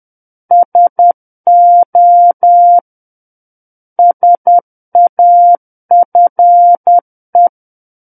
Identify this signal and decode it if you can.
Morse code, heard as a keyed tone, about 10 wpm, 700 Hz. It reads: SO SAFE
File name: Book One / 01-Chapter 1 - 10 WPM